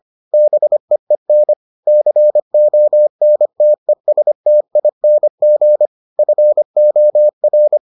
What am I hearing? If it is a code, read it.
Morse code, heard as a keyed tone, about 25 wpm, 600 Hz. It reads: BEEN CONTESTING FOR